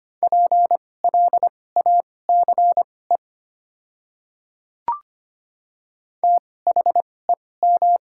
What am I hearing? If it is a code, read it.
Morse code, heard as a keyed tone, about 25 wpm, spaced as for 18 wpm, 690 Hz. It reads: PLACE THEM
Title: 2x Character Spacing - Top 200 Words - 25wpm